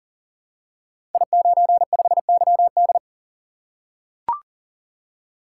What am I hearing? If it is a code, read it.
Morse code, heard as a keyed tone, about 43 wpm, 700 Hz. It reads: I95YD
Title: Code Groups - 5 Letter:Numbers - 40wpm